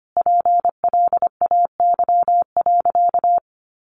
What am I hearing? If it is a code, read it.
Morse code, heard as a keyed tone, about 25 wpm, 700 Hz. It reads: PLAY.
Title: Easy Phrases 25wpm